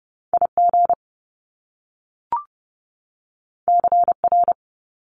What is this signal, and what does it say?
Morse code, heard as a keyed tone, about 30 wpm, 700 Hz. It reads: IG CR
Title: Top Two Letter Combinations in Words 30wpm